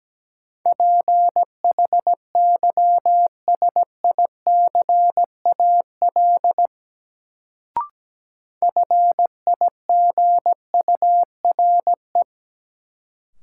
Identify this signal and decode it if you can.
Morse code, heard as a keyed tone, about 17 wpm, 700 Hz. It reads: PHYSICAL FIGURE
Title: Sentences from Top 1400 Words - New Word Drill Alt - 17wpm